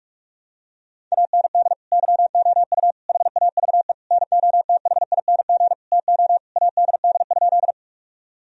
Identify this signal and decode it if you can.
Morse code, heard as a keyed tone, about 43 wpm, 700 Hz. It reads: AND YOU HAVE NOTHING TO ADD?
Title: Sentences from Top 1000 Words - Review Part 1 - 45wpm